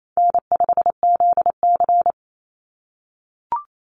Morse code, heard as a keyed tone, about 28 wpm, 700 Hz. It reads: N5ZC
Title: NA SOTA Chaser Callsigns 28wpm